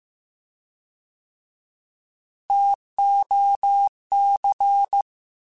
Morse code, heard as a keyed tone, about 15 wpm, 800 Hz. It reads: TOC
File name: Alphabetical / Alphabetical mp3s / Ts @13